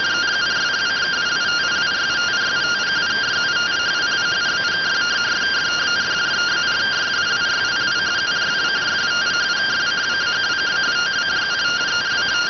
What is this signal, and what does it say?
Two-tone teleprinter signal, about 170 Hz shift.